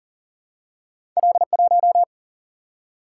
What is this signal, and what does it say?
Morse code, heard as a keyed tone, about 42 wpm, 700 Hz. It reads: L1
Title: Single-Letter Single-Number 40wpm